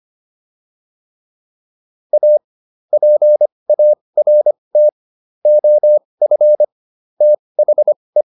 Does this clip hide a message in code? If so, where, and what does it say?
Yes: Morse code from 2.1 to 8.2 s, heard as a keyed tone, about 25 wpm, spaced as for 20 wpm, 600 Hz. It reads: A PART OF THE